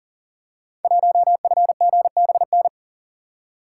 Morse code, heard as a keyed tone, about 41 wpm, 690 Hz. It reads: 1FGBN
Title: Code Groups - 5 Letter:Numbers - 40wpm